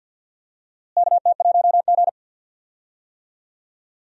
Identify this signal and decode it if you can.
Morse code, heard as a keyed tone, about 52 wpm, 700 Hz. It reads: KT1G
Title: USA Call Signs - 4 Characters - Repeated 2x - 50wpm